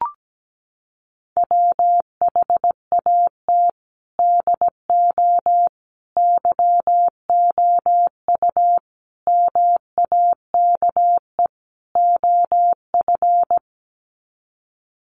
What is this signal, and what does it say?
Morse code, heard as a keyed tone, about 17 wpm, 700 Hz. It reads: WHAT DO YOU MAKE OF
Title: Sets of 5 Words 17wpm